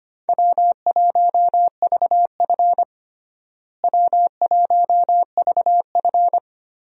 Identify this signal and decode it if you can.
Morse code, heard as a keyed tone, about 25 wpm, 700 Hz. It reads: W1VF W1VF
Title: USA Call Signs - 4 Characters - Repeated 3x - 25wpm